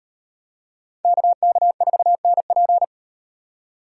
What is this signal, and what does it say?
Morse code, heard as a keyed tone, about 38 wpm, 700 Hz. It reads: KK4NP